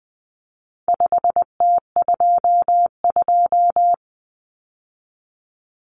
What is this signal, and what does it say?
Morse code, heard as a keyed tone, about 20 wpm, 700 Hz. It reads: 5T22
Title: Common Cut Numbers - 4-Digit 20wpm